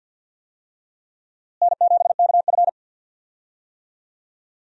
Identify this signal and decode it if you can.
Morse code, heard as a keyed tone, about 46 wpm, 700 Hz. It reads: N7KF
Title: USA Call Signs - 4 Characters - Repeated 3x - 50wpm